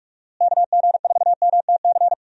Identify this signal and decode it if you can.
Morse code, heard as a keyed tone, about 43 wpm, 700 Hz. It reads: KG4MTC
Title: USA Call Signs - 6 Characters - Repeated 3x - 45wpm